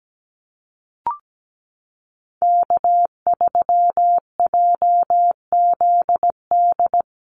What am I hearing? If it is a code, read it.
Morse code, heard as a keyed tone, about 17 wpm, 700 Hz. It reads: K3JZD
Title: NA SOTA Chaser Callsigns 17wpm